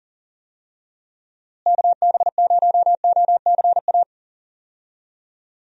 Morse code, heard as a keyed tone, about 40 wpm, 700 Hz. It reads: KB0OCA
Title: USA Call Signs - 6 Characters - Repeated 2x - 40wpm